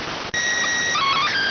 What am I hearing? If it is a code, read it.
SSTV (slow-scan television), PD 290.